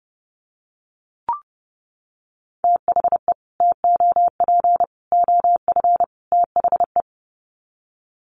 Morse code, heard as a keyed tone, about 30 wpm, 700 Hz. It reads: THE TOP OF THE
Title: Sets of 4 Words 30wpm